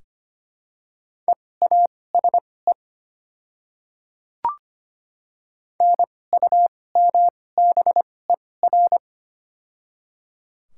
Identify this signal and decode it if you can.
Morse code, heard as a keyed tone, about 25 wpm, spaced as for 18 wpm, 690 Hz. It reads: EASE NUMBER
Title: 2x Character Spacing - Top 300 Words - 25wpm